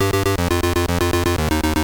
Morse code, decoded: none